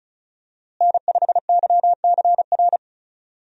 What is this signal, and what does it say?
Morse code, heard as a keyed tone, about 36 wpm, 700 Hz. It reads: N5YCR
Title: USA Call Signs - 5 Characters - Repeated 2x - 35wpm